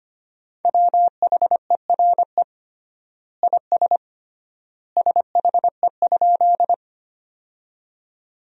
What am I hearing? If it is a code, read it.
Morse code, heard as a keyed tone, about 25 wpm, 700 Hz. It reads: WHERE IS SHE?
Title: 3x Word-Spacing - Sentences from Top 200 Words - 25wpm